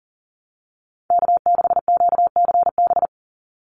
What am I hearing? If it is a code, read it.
Morse code, heard as a keyed tone, about 40 wpm, 700 Hz. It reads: K6QCB
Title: NA SOTA Chaser Callsigns 40wpm